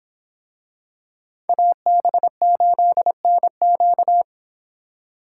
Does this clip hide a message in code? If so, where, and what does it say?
Yes: Morse code from 1.5 to 4.2 s, heard as a keyed tone, about 25 wpm, 700 Hz. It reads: AB8NQ